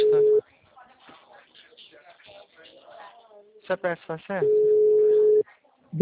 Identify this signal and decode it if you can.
Ringback tone (Europe, 425 Hz), with speech.